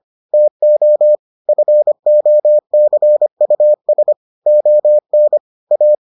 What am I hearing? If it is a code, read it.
Morse code, heard as a keyed tone, about 25 wpm, 600 Hz. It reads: TO FOCUS ON A